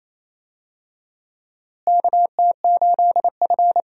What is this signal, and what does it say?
Morse code, heard as a keyed tone, about 28 wpm, 700 Hz. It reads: KT8F
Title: USA Call Signs - 4 Characters - Repeated 3x - 28wpm